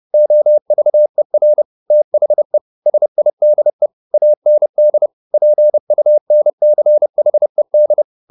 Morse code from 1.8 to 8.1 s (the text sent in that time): THE SIDE AND PUNCHED